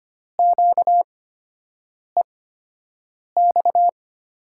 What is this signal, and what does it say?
Morse code, heard as a keyed tone, about 25 wpm, spaced as for 7 wpm, 700 Hz. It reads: QEX
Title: USA Call Signs - 6 Characters - 8x Character Spacing - 25wpm